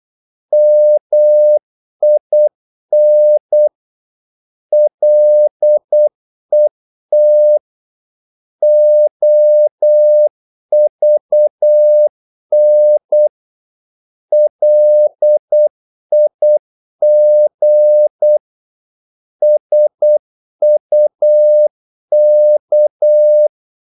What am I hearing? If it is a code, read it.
Morse code, heard as a keyed tone, about 8 wpm, 600 Hz. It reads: MIN LET OVN LIG SUK
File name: Korte ord DK 8wpm